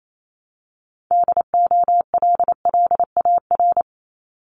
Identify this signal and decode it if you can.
Morse code, heard as a keyed tone, about 28 wpm, 700 Hz. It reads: DOLLAR
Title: Multi-letter Phonemes Repeated 3x 28wpm